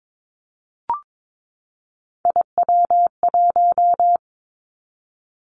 Morse code, heard as a keyed tone, about 22 wpm, 700 Hz. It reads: IW1
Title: Three Challenging Letter-Number Combinations 22wpm